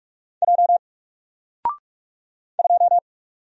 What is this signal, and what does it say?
Morse code, heard as a keyed tone, about 44 wpm, 700 Hz. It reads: J 2
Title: Single Character (Letter:Number:Punctuation:Prosign) Alt 45wpm